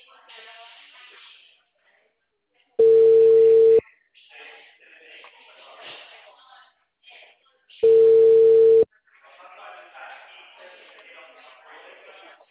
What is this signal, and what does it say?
Ringback tone (China, 450 Hz), with speech.
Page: Itu 2